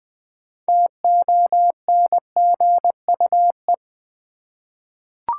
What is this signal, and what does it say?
Morse code, heard as a keyed tone, about 20 wpm, 700 Hz. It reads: TONGUE
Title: Six Letter Words - 20wpm